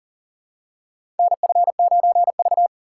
Morse code, heard as a keyed tone, about 40 wpm, 700 Hz. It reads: NF9V
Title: USA Call Signs - 4 Characters - 6x Character Spacing - 40wpm